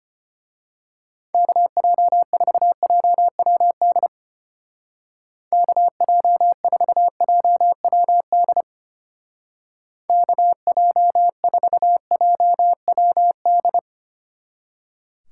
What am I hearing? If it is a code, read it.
Morse code, heard as a keyed tone, about 29 wpm, 700 Hz. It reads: KJ4JWD KJ4JWD KJ4JWD
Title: Speed Racing - USA 2x3 Call Signs 25wpm